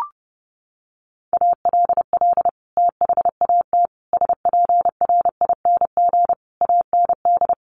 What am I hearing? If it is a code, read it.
Morse code, heard as a keyed tone, about 30 wpm, 700 Hz. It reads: ALL THAT SPRING AND
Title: Sentences from Top 1200 Words 30wpm